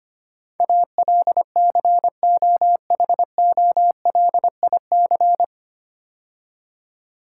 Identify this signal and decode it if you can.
Morse code, heard as a keyed tone, about 25 wpm, 700 Hz. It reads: ALCOHOLIC
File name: practice / 25wpm / 9 Letter Words - First 3 Letters Given - 25wpm